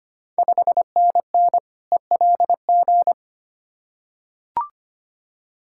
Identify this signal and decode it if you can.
Morse code, heard as a keyed tone, about 25 wpm, 690 Hz. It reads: 5NN ELG
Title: Ontario QSO Party Exchange 25wpm